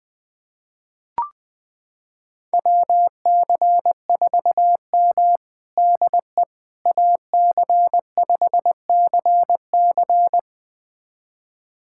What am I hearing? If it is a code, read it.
Morse code, heard as a keyed tone, about 20 wpm, 700 Hz. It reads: WC4M DE AC5CC
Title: USA Callsign DE Callsign 20wpm